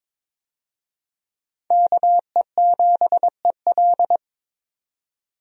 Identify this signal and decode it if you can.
Morse code, heard as a keyed tone, about 22 wpm, 700 Hz. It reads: KE7EL
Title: USA Call Signs - 5 Characters - 22wpm